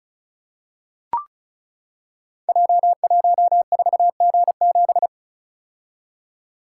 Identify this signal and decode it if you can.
Morse code, heard as a keyed tone, about 36 wpm, 700 Hz. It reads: J14G7
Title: Code Groups - 5 Letter:Numbers - 35wpm